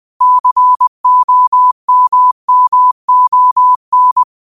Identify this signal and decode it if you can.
Morse code, heard as a keyed tone, about 20 wpm, 1000 Hz. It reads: COMMON